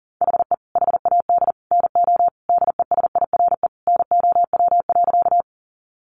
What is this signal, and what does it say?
Morse code, heard as a keyed tone, about 40 wpm, 700 Hz. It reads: HE HAD NO DESIRE NOW.
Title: Sentences from Top 1100 Words 40wpm